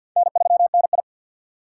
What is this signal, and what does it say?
Morse code, heard as a keyed tone, about 48 wpm, 700 Hz. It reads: N3NI
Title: USA Call Signs - 4 Characters - 3x Character Spacing - 50wpm